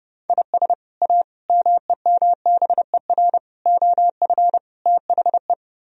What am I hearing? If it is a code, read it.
Morse code, heard as a keyed tone, about 38 wpm, 700 Hz. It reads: IS A MEMBER OF THE